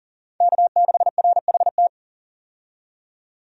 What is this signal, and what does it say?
Morse code, heard as a keyed tone, about 40 wpm, 700 Hz. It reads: K6RHT
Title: USA Call Signs - 5 Characters - 40wpm